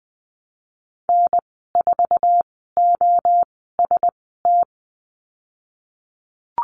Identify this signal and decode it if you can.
Morse code, heard as a keyed tone, about 20 wpm, spaced as for 15 wpm, 700 Hz. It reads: N4OST
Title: 088 Lesson 15g - Number 5 - Call Signs 20wpm